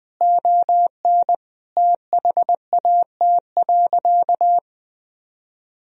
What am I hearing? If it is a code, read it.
Morse code, heard as a keyed tone, about 19 wpm, 700 Hz. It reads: ON THAT.
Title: Sentences from Top 1000 Words - Review Part 1 - 20wpm